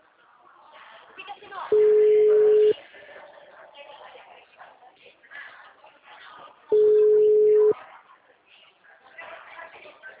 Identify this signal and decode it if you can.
Ringback tone (Europe, 425 Hz), with speech.